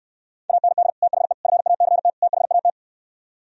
Morse code, heard as a keyed tone, about 68 wpm, 700 Hz. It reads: AND THE FACT THAT